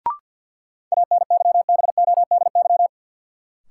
Morse code, heard as a keyed tone, about 52 wpm, 700 Hz. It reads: ANYBODY
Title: Sets of 5 Words - New Word Drill - 50wpm